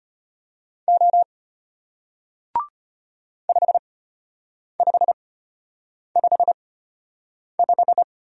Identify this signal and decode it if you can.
Morse code, heard as a keyed tone, about 33 wpm, spaced as for 8 wpm, 700 Hz. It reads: O 5555